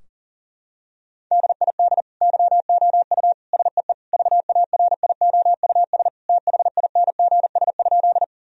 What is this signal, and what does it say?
Morse code, heard as a keyed tone, about 40 wpm, 700 Hz. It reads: DID YOU SEE VARIOUS THINGS?
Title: Sentences from Top 1000 Words - Review Part 4 - 40wpm